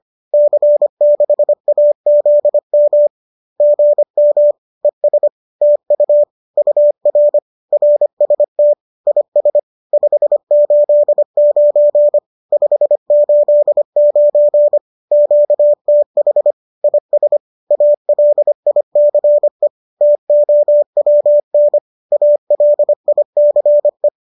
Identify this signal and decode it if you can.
Morse code, heard as a keyed tone, about 25 wpm, 590 Hz. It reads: C6AZM GM ES TU UR RST IS 589 589 QTH IS ALICE TOWN ALICE
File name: qso204 25